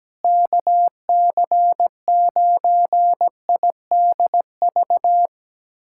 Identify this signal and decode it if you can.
Morse code, heard as a keyed tone, about 17 wpm, 700 Hz. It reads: KC9IDV
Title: USA Callsign DE Callsign - 6 Characters - 17wpm